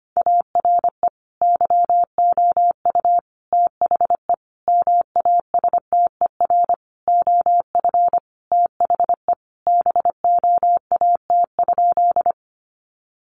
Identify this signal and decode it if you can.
Morse code, heard as a keyed tone, about 25 wpm, 700 Hz. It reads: ARE YOU THE MASTER OF THE BOAT?
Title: Sentences from Top 1300 Words 25wpm